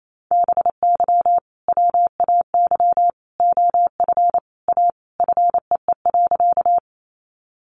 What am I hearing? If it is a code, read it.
Morse code, heard as a keyed tone, about 28 wpm, 700 Hz. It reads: BY WAY OF A FEE.
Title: Sentences from Top 1500 Words 28wpm